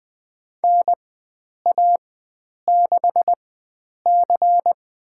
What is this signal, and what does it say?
Morse code, heard as a keyed tone, about 20 wpm, spaced as for 9 wpm, 700 Hz. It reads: NA6C
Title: USA Call Signs - 4 Characters - 4x Character Spacing - 20wpm